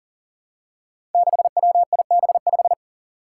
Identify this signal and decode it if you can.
Morse code, heard as a keyed tone, about 39 wpm, 700 Hz. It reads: 6WIB5
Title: Five Challenging Letter-Number Combinations 40wpm